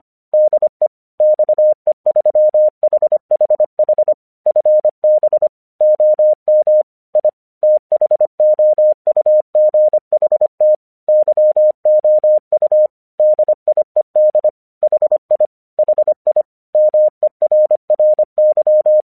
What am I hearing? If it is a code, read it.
Morse code, heard as a keyed tone, about 25 wpm, 610 Hz. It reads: DE XE3HHH FB OM I THOUGHT YOU DIED HI HI MERRY
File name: 1 English 25wpm / AQC31